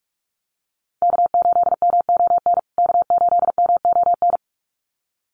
Morse code, heard as a keyed tone, about 44 wpm, 700 Hz. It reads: K8MON K8MON
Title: USA Call Signs Repeated 2x 45wpm